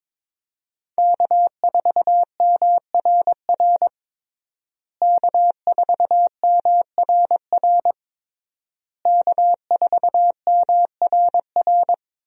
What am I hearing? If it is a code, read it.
Morse code, heard as a keyed tone, about 22 wpm, 700 Hz. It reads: K4MRR K4MRR K4MRR
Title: USA Call Signs - 5 Characters - Repeated 3x - 22wpm